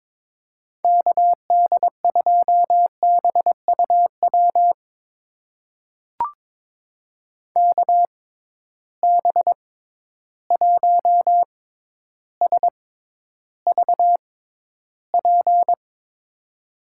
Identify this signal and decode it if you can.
Morse code, heard as a keyed tone, about 23 wpm, spaced as for 11 wpm, 700 Hz. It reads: KD2BUW KB1SVP
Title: USA Call Signs - 6 Characters - 6x Character Spacing - 22wpm